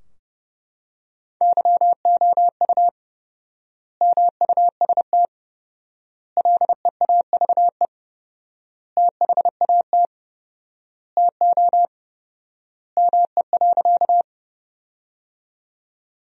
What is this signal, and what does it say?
Morse code, heard as a keyed tone, about 32 wpm, 700 Hz. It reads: YOU MUST LEAVE THAT TO ME.
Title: 4x Word-Spacing - Sentences from Top 400 Words - 30wpm